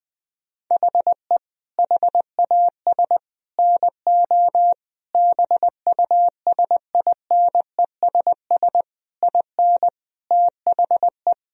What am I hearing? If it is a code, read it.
Morse code, heard as a keyed tone, about 20 wpm, 700 Hz. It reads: HE HAS NO BUSINESS IN THE
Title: Sentences from Top 1000 Words - Review Part 4 - 20wpm